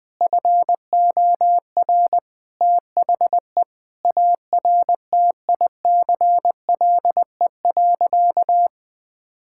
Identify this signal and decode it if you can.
Morse code, heard as a keyed tone, about 20 wpm, 700 Hz. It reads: FOR THE ARTICLE.
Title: Sentences from Top 1000 Words - Review Part 5 - 20wpm